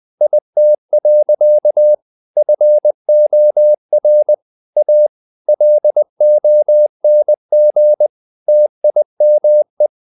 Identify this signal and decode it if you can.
Morse code, heard as a keyed tone, about 20 wpm, 600 Hz. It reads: IT. FOR A LONG TIME
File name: the old man and the sea / the old man and the sea part 1 20wpm